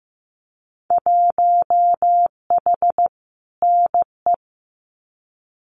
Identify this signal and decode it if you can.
Morse code, heard as a keyed tone, about 15 wpm, 700 Hz. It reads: 1H NE
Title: Winter Field Day Exchanges Encore 15wpm